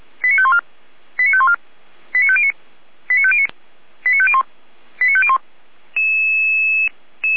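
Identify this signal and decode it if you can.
Five-tone selective call (ZVEI), heard as quick runs of five tones, tones of about 70 ms. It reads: 89514 89514 89509 89509 89501 89501
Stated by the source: ZVEI five-tone selective call received on SDR, code 89078.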